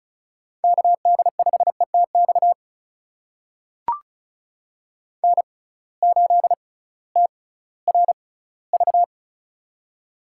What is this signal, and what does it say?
Morse code, heard as a keyed tone, about 36 wpm, spaced as for 19 wpm, 700 Hz. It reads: KD5ETX N8TRV
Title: USA Call Signs - 6x Character Spacing - 35wpm